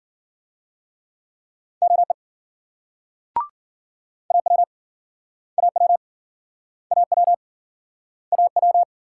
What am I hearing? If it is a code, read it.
Morse code, heard as a keyed tone, about 56 wpm, 700 Hz. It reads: OE AW AW AW AW